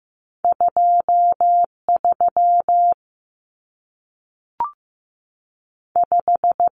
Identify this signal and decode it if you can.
Morse code, heard as a keyed tone, about 15 wpm, 700 Hz. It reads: 23 5
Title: Common Cut Numbers - 2-Digit 15wpm